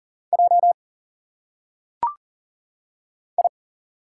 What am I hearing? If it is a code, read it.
Morse code, heard as a keyed tone, about 40 wpm, 700 Hz. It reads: J I
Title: Single Letters - Phonetic Pronunciation 40wpm